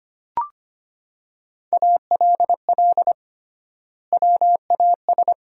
Morse code, heard as a keyed tone, about 25 wpm, 700 Hz. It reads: ALL WAS
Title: 3x Word-Spacing - Sentences from Top 500 Words - 25wpm